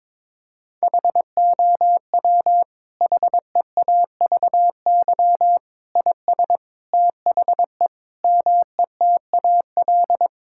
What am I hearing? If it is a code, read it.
Morse code, heard as a keyed tone, about 21 wpm, 700 Hz. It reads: HOW HEAVY IS THE METAL
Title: Sentences from Top 1000 Words - Review Part 6 - 22wpm